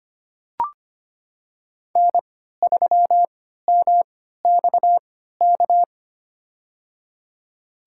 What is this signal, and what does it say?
Morse code, heard as a keyed tone, about 25 wpm, spaced as for 14 wpm, 700 Hz. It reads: N3MXK
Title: USA Call Signs - 5 Characters - 3x Character Spacing - 25wpm